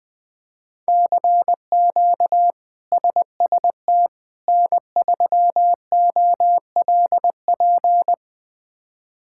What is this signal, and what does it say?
Morse code, heard as a keyed tone, about 21 wpm, 700 Hz. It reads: CQ SST N3OLP
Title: K1USN SST QSO Practice - 20wpm